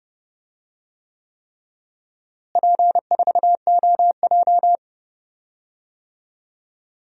Morse code, heard as a keyed tone, about 30 wpm, 700 Hz. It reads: P4OJ
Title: DXCC-100 Call Signs - 4 Characters - Repeated 3x - 30wpm